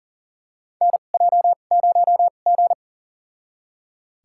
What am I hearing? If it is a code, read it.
Morse code, heard as a keyed tone, about 40 wpm, spaced as for 29 wpm, 700 Hz. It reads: NJ0G
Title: USA Call Signs - 4 Characters - 2x Character Spacing - 40wpm